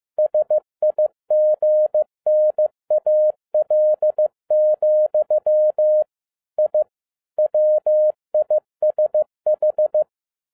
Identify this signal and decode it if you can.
Morse code, heard as a keyed tone, about 15 wpm, 610 Hz. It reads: SIGNAL, I WISH
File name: EXCITING QRP CW OPERATION 15wpm